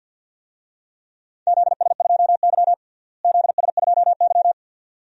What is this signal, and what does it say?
Morse code, heard as a keyed tone, about 52 wpm, 700 Hz. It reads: ZS2Y ZS2Y
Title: DXCC-100 Call Signs - 4 Characters - Repeated 2x - 50wpm